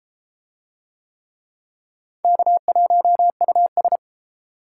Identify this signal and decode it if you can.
Morse code, heard as a keyed tone, about 32 wpm, 700 Hz. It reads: K1US